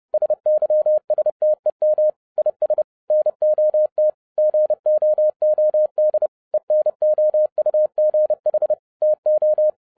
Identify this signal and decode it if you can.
Morse code, heard as a keyed tone, about 30 wpm, 610 Hz. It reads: SYSTEM IS NOT GOOD ENOUGH TO